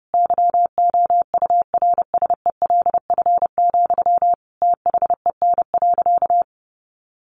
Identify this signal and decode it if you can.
Morse code, heard as a keyed tone, about 30 wpm, 700 Hz. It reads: YOURSELF, THEN.
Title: Sentences from Top 1100 Words 30wpm